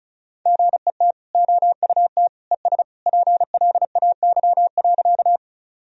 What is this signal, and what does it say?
Morse code, heard as a keyed tone, about 34 wpm, 700 Hz. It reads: GET OUT ES PLAY.
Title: Sentences from Top 200 Words and Abbreviations 35wpm